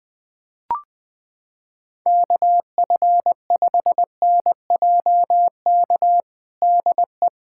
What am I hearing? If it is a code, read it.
Morse code, heard as a keyed tone, about 20 wpm, 700 Hz. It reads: KF5NJK DE
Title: USA Callsign DE Callsign - 6 Characters - 20wpm